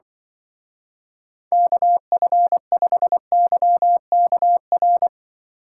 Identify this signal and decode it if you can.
Morse code, heard as a keyed tone, about 25 wpm, 700 Hz. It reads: KF5YKR
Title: Speed Racing - USA 2x3 Call Signs 20wpm